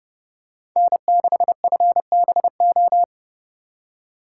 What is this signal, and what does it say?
Morse code, heard as a keyed tone, about 30 wpm, 700 Hz. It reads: N6FBO
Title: USA Call Signs - 5 Characters - 4x Character Spacing - 30wpm